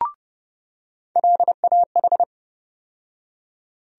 Morse code, heard as a keyed tone, about 32 wpm, 700 Hz. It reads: LAH
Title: USA Call Sign Postfixes - 3 Letters 30wpm